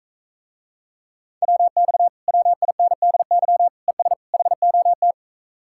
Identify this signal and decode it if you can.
Morse code, heard as a keyed tone, about 44 wpm, 700 Hz. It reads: WX WINDY ES HOT